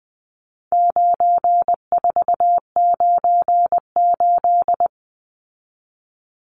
Morse code, heard as a keyed tone, about 20 wpm, 700 Hz. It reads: 9498
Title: Numbers - 4-Digit 20wpm